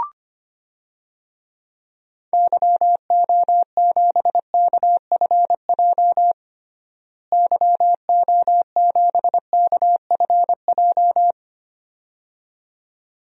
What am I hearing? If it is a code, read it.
Morse code, heard as a keyed tone, about 25 wpm, 700 Hz. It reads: YO7KFJ YO7KFJ
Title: DXCC-100 Call Signs - 6 Characters - Repeated 2x - 25wpm